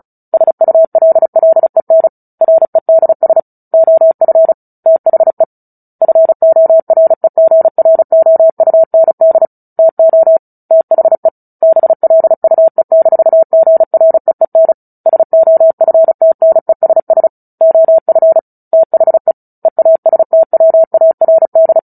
Morse code, heard as a keyed tone, about 34 wpm, 650 Hz. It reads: SULLEN REDS OF THE FOREGROUND TO THE BLUE-GREEN SOFTNESS OF THE EASTWARD
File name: Book Two / 09-Chapter 9 - 35 WPM